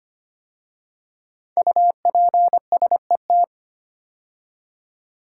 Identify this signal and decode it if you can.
Morse code, heard as a keyed tone, about 25 wpm, 700 Hz. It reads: UPSET
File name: 5 Letter Words - First 3 Letters Given - 25wpm